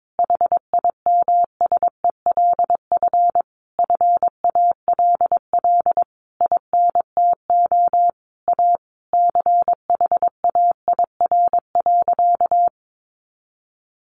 Morse code, heard as a keyed tone, about 22 wpm, 700 Hz. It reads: HIMSELF FALL INTO A CHAIR.
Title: A Sound Of Thunder 22wpm